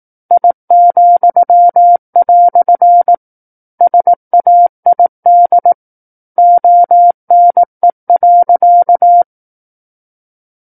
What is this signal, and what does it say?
Morse code, heard as a keyed tone, about 18 wpm, 700 Hz. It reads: I," SAID ONE.
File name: 18 WPM